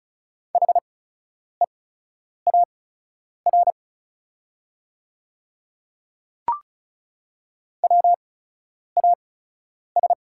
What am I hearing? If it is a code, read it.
Morse code, heard as a keyed tone, about 34 wpm, spaced as for 10 wpm, 700 Hz. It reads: HEAR WAS
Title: 8x Character Spacing - Top 300 Words - 35wpm